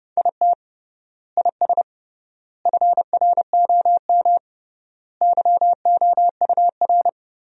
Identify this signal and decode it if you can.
Morse code, heard as a keyed tone, about 32 wpm, 700 Hz. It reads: IT IS FROM YOUR
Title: 3x Word-Spacing - Sentences from Top 300 Words - 30wpm